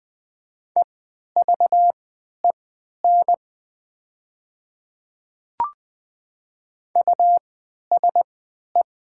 Morse code, heard as a keyed tone, about 20 wpm, spaced as for 11 wpm, 700 Hz. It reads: EVEN USE
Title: 3x Character Spacing - Top 500 Words - 20wpm